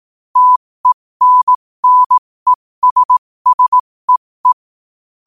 Morse code, heard as a keyed tone, about 18 wpm, spaced as for 15 wpm, 1000 Hz. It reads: TENNESSEE
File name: qotdfiles / 15wpm